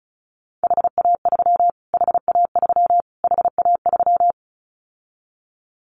Morse code, heard as a keyed tone, about 34 wpm, 700 Hz. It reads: HA3 HA3 HA3
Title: Callsign Prefixes for Top 25 DXCC - Repeated 3x - 35wpm